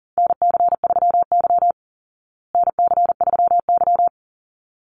Morse code, heard as a keyed tone, about 40 wpm, 700 Hz. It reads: NC3Y NC3Y
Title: Call Signs Repeated 3x - 40wpm